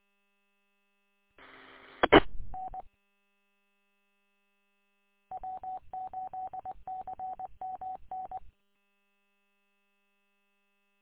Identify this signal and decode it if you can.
Morse code, heard as a keyed tone, about 20 wpm, 750 Hz. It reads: I N W8CMN
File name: w8cmn- -combined